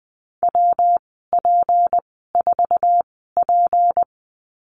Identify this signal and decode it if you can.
Morse code, heard as a keyed tone, about 20 wpm, spaced as for 14 wpm, 700 Hz. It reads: WP4P
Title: 173 Lesson 28g - Number 9 - Call Signs 20wpm